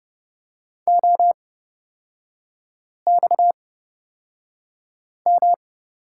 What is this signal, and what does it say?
Morse code, heard as a keyed tone, about 31 wpm, spaced as for 5 wpm, 690 Hz. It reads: OXM
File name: Lesson 37e - Letter x - Three Characters 30wpm